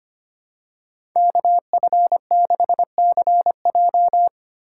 Morse code, heard as a keyed tone, about 25 wpm, 700 Hz. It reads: KF6CJ
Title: USA Call Signs - 2x Character Spacing - 25wpm